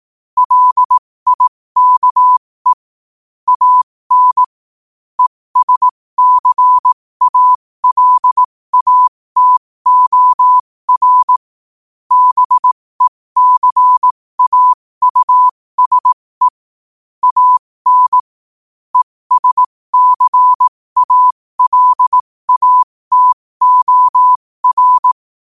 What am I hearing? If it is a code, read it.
Morse code, heard as a keyed tone, about 18 wpm, spaced as for 15 wpm, 1000 Hz. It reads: LIKE AN ESCALATOR BECAUSE AN ESCALATOR
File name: qotdfiles / 15wpm